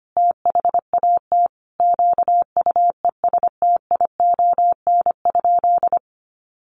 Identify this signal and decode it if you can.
Morse code, heard as a keyed tone, about 25 wpm, 700 Hz. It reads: THAT QUESTION?
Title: Sentences from Top 400 Words 25wpm